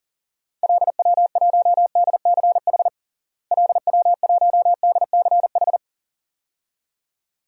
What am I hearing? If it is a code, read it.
Morse code, heard as a keyed tone, about 40 wpm, 700 Hz. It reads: LW1DCH LW1DCH
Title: DXCC-100 Call Signs - 6 Characters - Repeated 2x - 40wpm